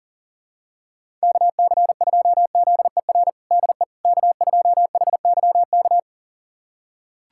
Morse code, heard as a keyed tone, about 40 wpm, 700 Hz. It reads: KC2ZER DE K2HYK
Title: USA Callsign DE Callsign 40wpm